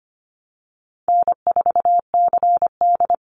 Morse code, heard as a keyed tone, about 25 wpm, 700 Hz. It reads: N4CD
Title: Easy Call Signs Repeated 3x - 25wpm